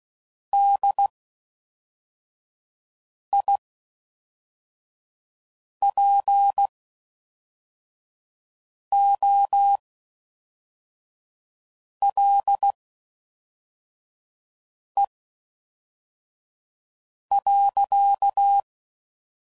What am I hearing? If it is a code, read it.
Morse code, heard as a keyed tone, about 16 wpm, spaced as for 4 wpm, 800 Hz. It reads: DIPOLE.